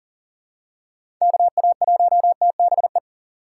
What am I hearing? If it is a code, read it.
Morse code, heard as a keyed tone, about 40 wpm, 700 Hz. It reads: KA1TBE
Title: USA Call Signs - 6 Characters - 2x Character Spacing - 40wpm